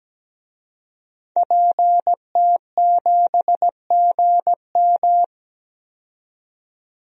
Morse code, heard as a keyed tone, about 17 wpm, 700 Hz. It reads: PT7GM
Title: DXCC-100 Call Signs - 5 Characters - 17wpm